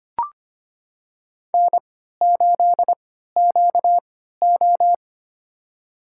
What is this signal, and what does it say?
Morse code, heard as a keyed tone, about 25 wpm, spaced as for 14 wpm, 700 Hz. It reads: N8QO
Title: USA Call Signs - 4 Characters - 3x Character Spacing - 25wpm